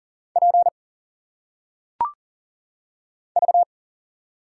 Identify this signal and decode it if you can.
Morse code, heard as a keyed tone, about 41 wpm, 700 Hz. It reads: P V
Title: Single Letters - Phonetic Pronunciation 40wpm